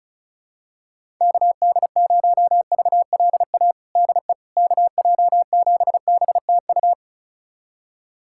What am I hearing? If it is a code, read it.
Morse code, heard as a keyed tone, about 36 wpm, 700 Hz. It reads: KD0VLA DE KJ7BTU
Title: USA Callsign DE Callsign - 6 Characters - 35wpm